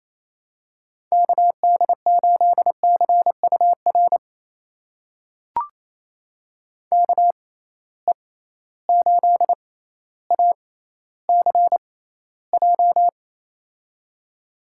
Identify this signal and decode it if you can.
Morse code, heard as a keyed tone, about 28 wpm, spaced as for 14 wpm, 700 Hz. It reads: KD8CUR KE8ACJ
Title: USA Call Signs - 6 Characters - 6x Character Spacing - 28wpm